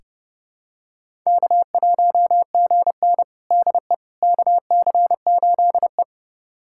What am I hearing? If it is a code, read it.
Morse code, heard as a keyed tone, about 30 wpm, 700 Hz. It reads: K1GN DE KC8E
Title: USA Callsign DE Callsign - 4 Characters - 30wpm